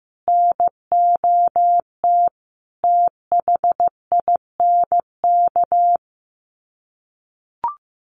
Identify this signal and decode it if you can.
Morse code, heard as a keyed tone, about 15 wpm, 700 Hz. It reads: NOT THINK
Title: Sets of 5 Words 15wpm